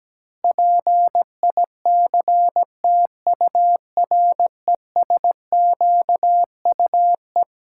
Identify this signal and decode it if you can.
Morse code, heard as a keyed tone, about 17 wpm, 700 Hz. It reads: PICTURESQUE
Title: Pangram Challenge v2 - 17wpm